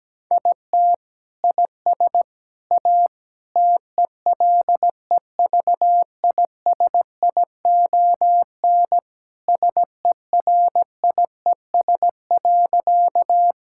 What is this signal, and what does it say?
Morse code, heard as a keyed tone, about 17 wpm, 700 Hz. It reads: IT IS A TELEVISION SERIES.
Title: Sentences from Top 1000 Words - Review Part 5 - 17wpm